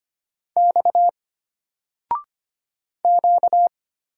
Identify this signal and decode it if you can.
Morse code, heard as a keyed tone, about 25 wpm, 700 Hz. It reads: X Q
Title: Single Letter-Number 25wpm